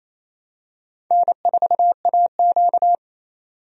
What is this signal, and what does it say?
Morse code, heard as a keyed tone, about 28 wpm, 700 Hz. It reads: N4AQ
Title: USA Call Signs - 4 Characters - 28wpm